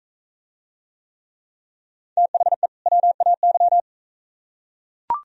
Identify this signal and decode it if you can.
Morse code, heard as a keyed tone, about 41 wpm, 700 Hz. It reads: THE WAY